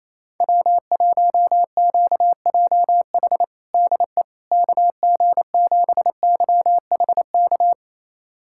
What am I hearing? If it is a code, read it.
Morse code, heard as a keyed tone, about 28 wpm, 700 Hz. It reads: W1QJH DE KG7YHK
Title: USA Callsign DE Callsign 28wpm